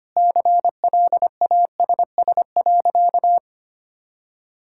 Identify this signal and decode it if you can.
Morse code, heard as a keyed tone, about 25 wpm, 700 Hz. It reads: CLASS.
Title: Sentences from Top 1000 Words - Review Part 6 - 25wpm